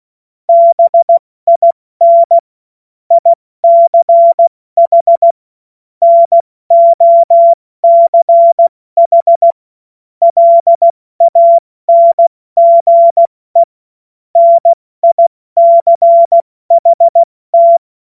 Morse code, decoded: BIN ICH NOCH LANGE NICHT

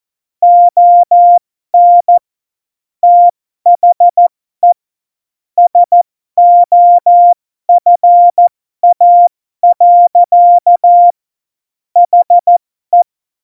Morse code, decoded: ON THE SOFA. HE

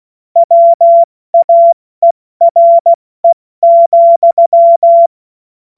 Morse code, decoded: WAERE,

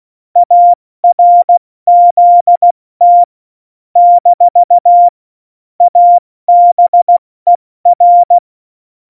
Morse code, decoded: ARZT - ABER

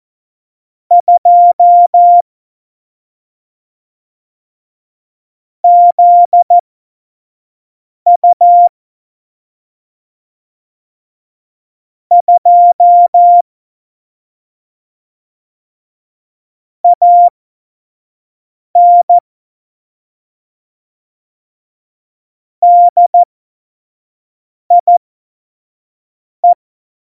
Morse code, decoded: 2 ZU 2 AN DIE

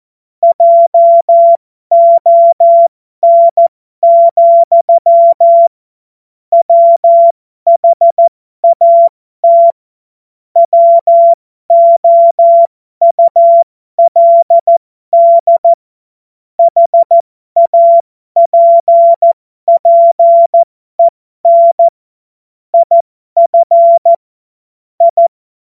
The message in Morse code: JON, WHAT WOULD HAPPEN IF I

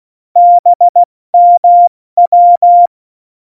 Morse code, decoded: BMW

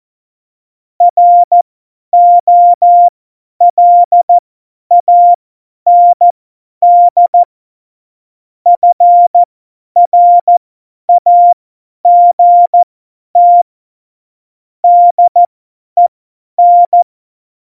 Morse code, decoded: ROLAND FRAGT DEN

